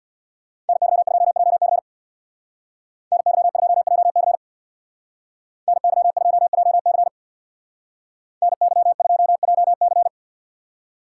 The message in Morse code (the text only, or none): NY2JC NY2JC NY2JC NY2JC